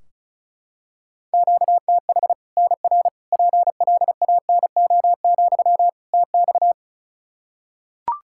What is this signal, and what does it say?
Morse code, heard as a keyed tone, about 36 wpm, 700 Hz. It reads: QTH NR PLANO, TX
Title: Practice QSO - Exchange 1 - Beginning - 35wpm